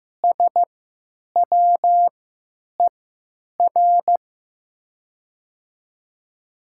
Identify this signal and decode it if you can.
Morse code, heard as a keyed tone, about 15 wpm, spaced as for 9 wpm, 700 Hz. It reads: SWER
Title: 3x Character Spacing - Top 300 Words - 15wpm